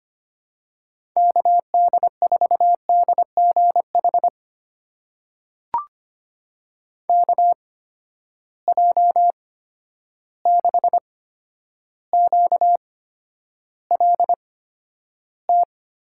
Morse code, heard as a keyed tone, about 25 wpm, spaced as for 11 wpm, 700 Hz. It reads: KD4DGH KJ6QLT
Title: USA Call Signs - 6 Characters - 8x Character Spacing - 25wpm